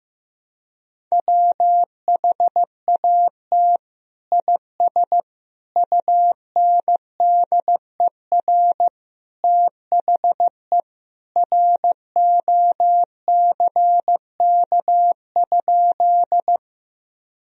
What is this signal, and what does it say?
Morse code, heard as a keyed tone, about 15 wpm, 700 Hz. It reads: WHAT IS UNDER THE ROCK?
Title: Sentences from Top 1000 Words - Review Part 2 - 15wpm